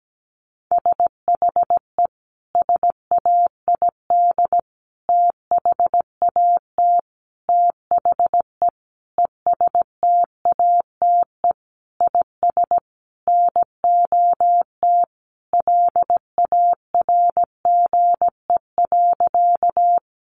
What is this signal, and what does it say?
Morse code, heard as a keyed tone, about 17 wpm, 700 Hz. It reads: SHE SAID THAT THE ESTATE IS NOT LARGE.
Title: Sentences from Top 1200 Words 17wpm